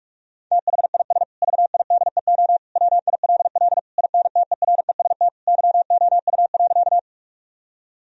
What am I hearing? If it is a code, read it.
Morse code, heard as a keyed tone, about 46 wpm, 700 Hz. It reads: THIS VIDEO WILL INTEREST YOU.
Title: Sentences from Top 1000 Words - Review Part 4 - 45wpm